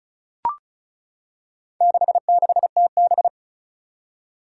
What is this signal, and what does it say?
Morse code, heard as a keyed tone, about 36 wpm, 700 Hz. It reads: 66TB
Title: Four Challenging Letter-Number Combinations 35wpm